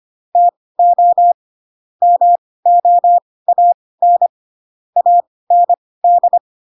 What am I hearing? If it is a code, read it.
Morse code, heard as a keyed tone, about 25 wpm, spaced as for 18 wpm, 700 Hz. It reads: TO MOAN AND
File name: Pinocchio / Pinocchio ch1 12 25 ik1hge